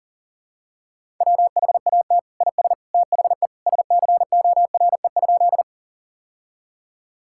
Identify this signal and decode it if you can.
Morse code, heard as a keyed tone, about 40 wpm, 700 Hz. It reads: WHAT IS THE SCORE?
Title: Sentences from Top 1600 Words 40wpm